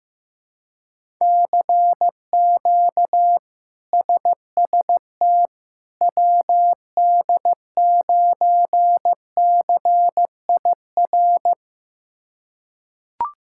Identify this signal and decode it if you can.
Morse code, heard as a keyed tone, about 15 wpm, 700 Hz. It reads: CQ SST WD9CIR
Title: K1USN SST QSO Practice - 15wpm